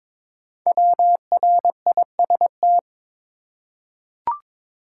Morse code, heard as a keyed tone, about 22 wpm, 700 Hz. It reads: WRIST
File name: 5 Letter Words - First 3 Letters Given - 22wpm